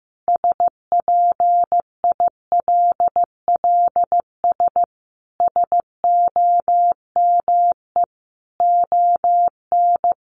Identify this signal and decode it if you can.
Morse code, heard as a keyed tone, about 15 wpm, 700 Hz. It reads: SPILLS SOME ON
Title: Additional Easy Phrases 15wpm